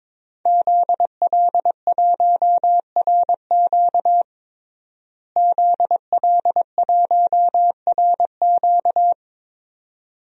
Morse code, heard as a keyed tone, about 22 wpm, 700 Hz. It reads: ZL1RQ ZL1RQ
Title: DXCC-100 Call Signs - 5 Characters - Repeated 2x - 22wpm